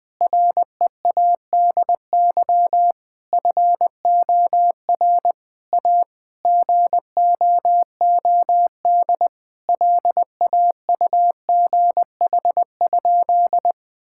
Morse code, heard as a keyed tone, about 20 wpm, 700 Hz. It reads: READY FOR A GOOD LAUGH?
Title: Sentences from Top 1000 Words - Review Part 5 - 20wpm